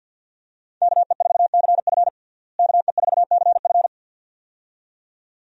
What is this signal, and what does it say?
Morse code, heard as a keyed tone, about 53 wpm, 700 Hz. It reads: KE4CF KE4CF
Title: USA Call Signs - 5 Characters - Repeated 2x - 50wpm